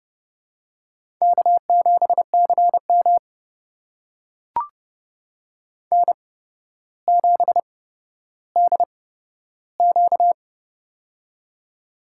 Morse code, heard as a keyed tone, about 30 wpm, spaced as for 13 wpm, 700 Hz. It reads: K7CM N7DQ
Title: USA Call Signs - 4 Characters - 8x Character Spacing - 30wpm